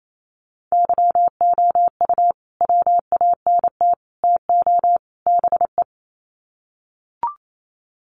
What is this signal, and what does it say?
Morse code, heard as a keyed tone, about 28 wpm, 700 Hz. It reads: YOU WANT TO BE
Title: Sets of 4 Words 28wpm